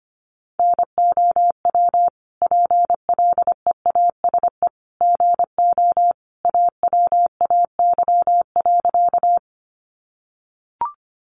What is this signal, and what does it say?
Morse code, heard as a keyed tone, about 25 wpm, 700 Hz. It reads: NOW PLEASE GO AWAY.
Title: Sentences from Top 1500 Words 25wpm